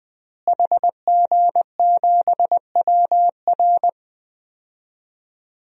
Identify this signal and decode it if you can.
Morse code, heard as a keyed tone, about 20 wpm, 700 Hz. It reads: HG7WR
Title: DXCC-100 Call Signs - 5 Characters - Repeated 2x - 20wpm